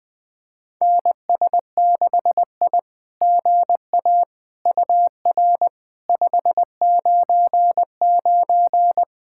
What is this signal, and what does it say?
Morse code, heard as a keyed tone, about 20 wpm, 700 Hz. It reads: NS6I GA UR 599
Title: SOTA QSO Practice 20wpm